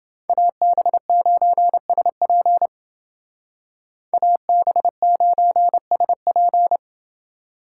Morse code, heard as a keyed tone, about 29 wpm, 700 Hz. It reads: AB9SP AB9SP